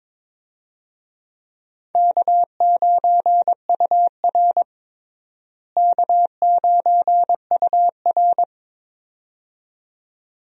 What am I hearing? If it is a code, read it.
Morse code, heard as a keyed tone, about 21 wpm, 700 Hz. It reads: K9UR K9UR
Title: USA Call Signs - 4 Characters - Repeated 2x - 22wpm